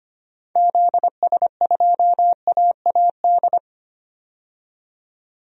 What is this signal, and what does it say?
Morse code, heard as a keyed tone, about 25 wpm, 700 Hz. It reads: ZS2AAD
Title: DXCC-100 Call Signs - 6 Characters - 25wpm